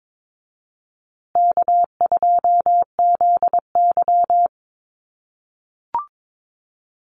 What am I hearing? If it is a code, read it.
Morse code, heard as a keyed tone, about 22 wpm, 700 Hz. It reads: K2ZY
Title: USA Call Signs Repeated 2x 22wpm